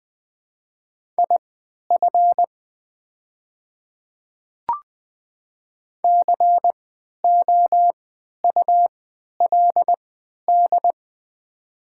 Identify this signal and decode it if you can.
Morse code, heard as a keyed tone, about 20 wpm, spaced as for 11 wpm, 700 Hz. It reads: IF COULD
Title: 3x Character Spacing - Top 300 Words - 20wpm